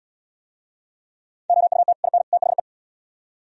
Morse code, heard as a keyed tone, about 77 wpm, 700 Hz. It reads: OUT IN THE